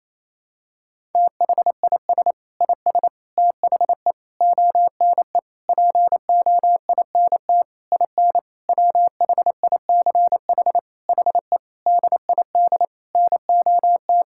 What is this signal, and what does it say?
Morse code, heard as a keyed tone, about 28 wpm, 700 Hz. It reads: THIS IS THE ONE POINT IN WHICH HE DID NOT
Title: Sentences from Top 1000 Words - Review Part 5 - 28wpm